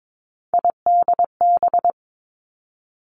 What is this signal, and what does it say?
Morse code, heard as a keyed tone, about 22 wpm, 700 Hz. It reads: IDB
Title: Three Challenging Letter-Number Combinations 22wpm